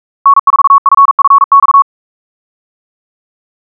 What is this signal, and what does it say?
Morse code, heard as a keyed tone, about 42 wpm, 1110 Hz. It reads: N4FFX
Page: SampleHST1.wav <-- 158cpm (=47wpm(paris)) 1111Hz sinusoidal waveform